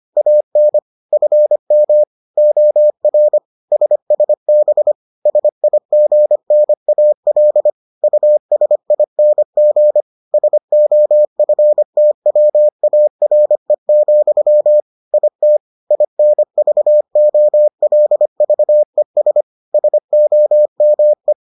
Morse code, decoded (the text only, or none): AN FM OR SSB SIGNAL USING SOFTWARE, IT INVOLVES SOME